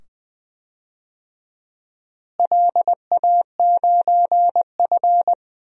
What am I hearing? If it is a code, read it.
Morse code, heard as a keyed tone, about 20 wpm, 700 Hz. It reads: LA9F
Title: DXCC-100 Call Signs - 4 Characters - Repeated 2x - 20wpm